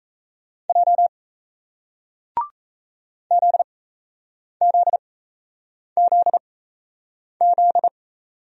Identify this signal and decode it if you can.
Morse code, heard as a keyed tone, about 38 wpm, spaced as for 8 wpm, 700 Hz. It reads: J ZZZZ